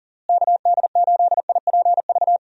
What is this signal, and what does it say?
Morse code, heard as a keyed tone, about 37 wpm, 700 Hz. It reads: KD8IPV
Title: USA Call Signs - 6 Characters - Repeated 2x - 40wpm